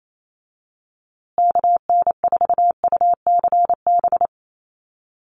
Morse code, heard as a keyed tone, about 28 wpm, 700 Hz. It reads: KN4UCB
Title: USA Call Signs Repeated 2x 28wpm